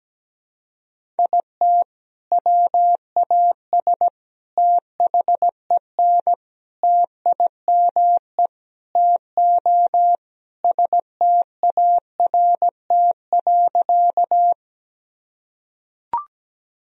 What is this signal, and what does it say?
Morse code, heard as a keyed tone, about 17 wpm, 700 Hz. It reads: IT WAS THEN TIME TO START.
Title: Sentences from Top 1000 Words - Review Part 1 - 17wpm